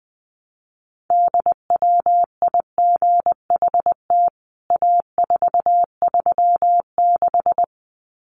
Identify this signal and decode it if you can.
Morse code, heard as a keyed tone, about 20 wpm, 700 Hz. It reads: DWIGHT A436
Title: CWT Exchange 20wpm